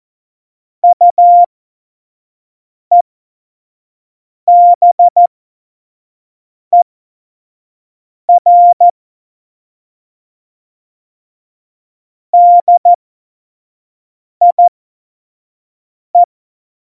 Morse code, heard as a keyed tone, about 14 wpm, spaced as for 5 wpm, 700 Hz. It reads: UEBER DIE
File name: Morsekurs / Witze / Witz 17 April 2018, 25 F70